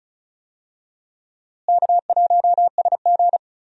Morse code, heard as a keyed tone, about 36 wpm, 700 Hz. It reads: K1SG
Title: USA Call Signs - 4 Characters - Repeated 2x - 35wpm